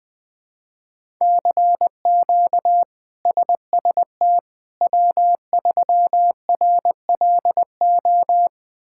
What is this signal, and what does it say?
Morse code, heard as a keyed tone, about 20 wpm, 700 Hz. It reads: CQ SST W3RLO
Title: K1USN SST QSO Practice - 20wpm